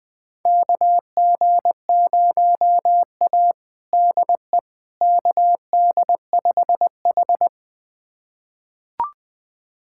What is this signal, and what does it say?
Morse code, heard as a keyed tone, about 20 wpm, 700 Hz. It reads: KG0A DE KD5H
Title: USA Callsign DE Callsign - 4 Characters - 20wpm